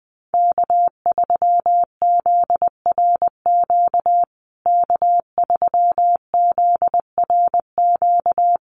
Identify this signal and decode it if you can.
Morse code, heard as a keyed tone, about 20 wpm, 700 Hz. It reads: K3ZRQ K3ZRQ
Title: USA Call Signs Repeated 3x 20wpm